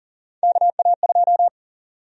Morse code, heard as a keyed tone, about 40 wpm, 700 Hz. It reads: KA2
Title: USA Call Sign Prefixes - 40wpm